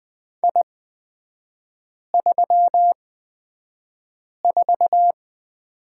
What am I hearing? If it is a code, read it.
Morse code, heard as a keyed tone, about 20 wpm, spaced as for 5 wpm, 690 Hz. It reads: I34
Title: 229 Lesson 37e - Letter x - Three Characters 20wpm